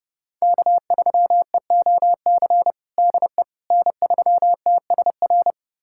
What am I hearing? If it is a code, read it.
Morse code, heard as a keyed tone, about 30 wpm, 700 Hz. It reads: K3EOC DE N3TSR
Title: USA Callsign DE Callsign - 5 Characters - 30wpm